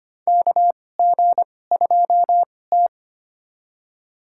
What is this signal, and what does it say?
Morse code, heard as a keyed tone, about 25 wpm, spaced as for 18 wpm, 700 Hz. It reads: KG2T
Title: USA Call Signs - 4 Characters - 2x Character Spacing - 25wpm